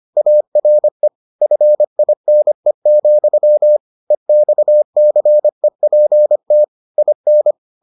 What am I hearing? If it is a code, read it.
Morse code, heard as a keyed tone, about 25 wpm, 600 Hz. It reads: ARE FINE, EXCEPT IN